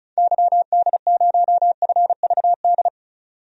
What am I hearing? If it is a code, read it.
Morse code, heard as a keyed tone, about 36 wpm, 700 Hz. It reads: YD0FVD
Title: DXCC-100 Call Signs - 6 Characters - 35wpm